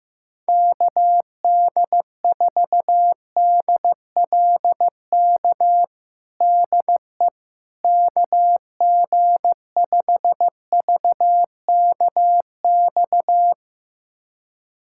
Morse code, heard as a keyed tone, about 15 wpm, 700 Hz. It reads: KD4DLK DE KG5VKX
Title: USA Callsign DE Callsign - 6 Characters - 15wpm